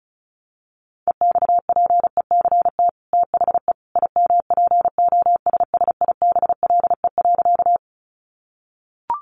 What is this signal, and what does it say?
Morse code, heard as a keyed tone, about 34 wpm, 700 Hz. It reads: EXPECT THE IMPOSSIBLE.
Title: Sentences from Top 1100 Words 35wpm